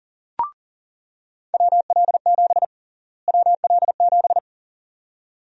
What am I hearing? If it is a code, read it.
Morse code, heard as a keyed tone, about 40 wpm, 700 Hz. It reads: WL7 WL7
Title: USA Call Sign Prefixes - Repeated 2x - 40wpm